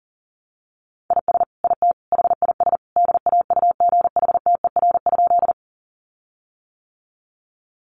Morse code, heard as a keyed tone, about 40 wpm, 700 Hz. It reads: IS IT HIS DAUGHTER?
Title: Sentences from Top 1500 Words 40wpm